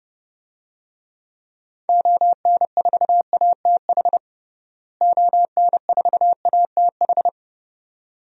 Morse code, heard as a keyed tone, about 30 wpm, 700 Hz. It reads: ON4ATH ON4ATH
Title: DXCC-100 Call Signs - 6 Characters - Repeated 2x - 30wpm